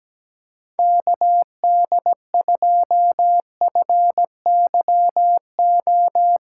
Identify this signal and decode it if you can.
Morse code, heard as a keyed tone, about 17 wpm, 700 Hz. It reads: KD2FYO
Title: USA Call Signs - 6 Characters - 3x Character Spacing - 17wpm